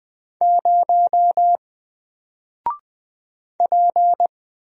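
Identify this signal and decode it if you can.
Morse code, heard as a keyed tone, about 20 wpm, 700 Hz. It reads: 0 P
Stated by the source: Single letter and number Morse drill at 20 wpm, from N4LCD.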